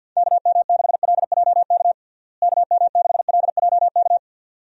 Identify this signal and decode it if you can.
Morse code, heard as a keyed tone, about 51 wpm, 700 Hz. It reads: KM6LJK KM6LJK
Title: USA Call Signs - 6 Characters - Repeated 2x - 50wpm